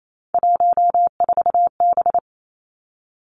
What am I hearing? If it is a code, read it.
Morse code, heard as a keyed tone, about 28 wpm, 700 Hz. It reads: 14B
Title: Three Challenging Letter-Number Combinations 28wpm